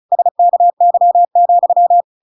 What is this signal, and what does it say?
Morse code, heard as a keyed tone, about 34 wpm, 700 Hz. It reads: SKY,
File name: Pinocchio / Pinocchio ch6 35 35 ik1hge